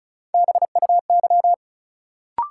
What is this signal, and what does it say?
Morse code, heard as a keyed tone, about 36 wpm, 700 Hz. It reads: BUY
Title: Sentences from Top 700 Words - New Word Drill - 35wpm